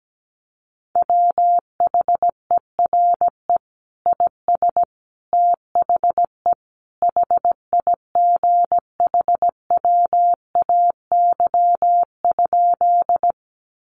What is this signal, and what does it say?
Morse code, heard as a keyed tone, about 17 wpm, 700 Hz. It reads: WHERE IS THE HIGHWAY?
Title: Sentences from Top 1400 Words 17wpm